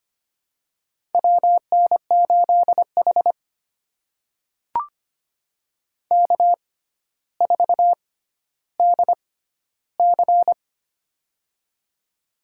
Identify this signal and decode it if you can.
Morse code, heard as a keyed tone, about 25 wpm, spaced as for 13 wpm, 700 Hz. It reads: WN8H K4DC
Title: USA Call Signs - 4 Characters - 6x Character Spacing - 25wpm